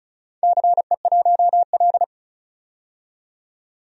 Morse code, heard as a keyed tone, about 36 wpm, 700 Hz. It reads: CE1L
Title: DXCC-100 Call Signs - 4 Characters - 35wpm